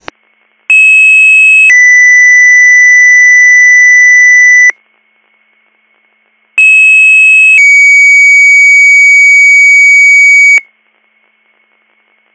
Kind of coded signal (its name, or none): two-tone page